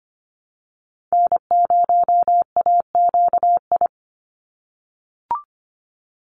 Morse code, heard as a keyed tone, about 24 wpm, 700 Hz. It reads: N0AQI
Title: USA Call Signs 25wpm